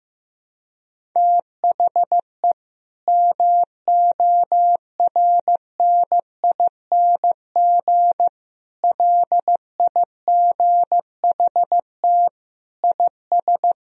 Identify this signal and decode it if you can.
Morse code, heard as a keyed tone, about 16 wpm, 700 Hz. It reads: THE MORNING LIGHT IS
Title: Sentences from Top 1000 Words - Review Part 1 - 15wpm